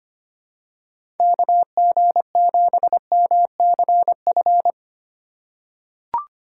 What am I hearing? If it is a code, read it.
Morse code, heard as a keyed tone, about 25 wpm, 700 Hz. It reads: KG7MCF
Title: USA Call Signs - 6 Characters - 3x Character Spacing - 25wpm